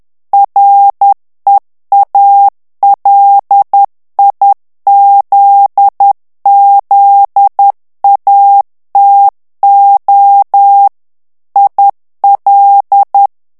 Morse code, heard as a keyed tone, about 10 wpm, 800 Hz. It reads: REALIZZATO IL
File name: ITALIEN texte clair 2 vitesse 10 mots minute